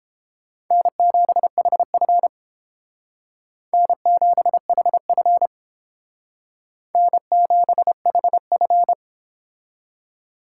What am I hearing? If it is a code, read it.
Morse code, heard as a keyed tone, about 28 wpm, 700 Hz. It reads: N7HF N7HF N7HF